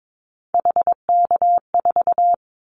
Morse code, heard as a keyed tone, about 22 wpm, 700 Hz. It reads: HK4
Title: Callsign Prefixes for Top 100 DXCC - 22wpm